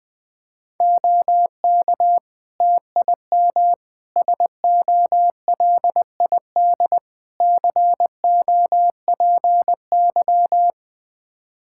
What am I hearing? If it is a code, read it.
Morse code, heard as a keyed tone, about 21 wpm, 700 Hz. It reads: OK TIM SOLID COPY
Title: Practice QSO - Exchange 3 - More Info - 20wpm